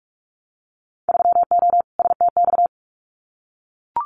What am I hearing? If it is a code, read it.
Morse code, heard as a keyed tone, about 44 wpm, 700 Hz. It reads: 3O STX
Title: Winter Field Day Exchanges 45wpm